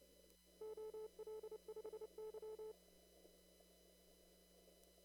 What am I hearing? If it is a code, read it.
Morse code, heard as a keyed tone, about 30 wpm, 430 Hz. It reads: OL5Y
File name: CW / kt5j wpx cw-op-k5pi